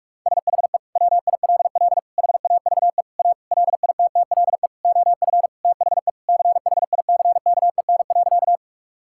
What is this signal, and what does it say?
Morse code, heard as a keyed tone, about 44 wpm, 700 Hz. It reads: SHE WILL HAVE A LITTLE OF THE CHICKEN.
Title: Sentences from Top 1000 Words - Review Part 5 - 45wpm